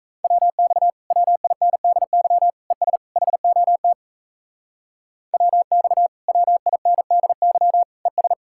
Morse code, heard as a keyed tone, about 43 wpm, 700 Hz. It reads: WX WINDY ES HOT WX WINDY ES